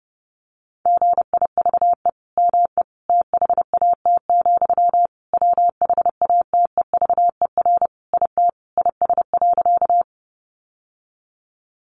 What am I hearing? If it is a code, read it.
Morse code, heard as a keyed tone, about 30 wpm, 700 Hz. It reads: GIVE ME THAT, WHATEVER IT IS.
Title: Sentences from Top 1100 Words 30wpm